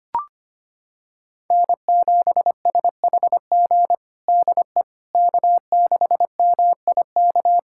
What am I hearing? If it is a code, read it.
Morse code, heard as a keyed tone, about 25 wpm, 700 Hz. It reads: N7SHG DE K6MIK
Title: USA Callsign DE Callsign - 5 Characters - 25wpm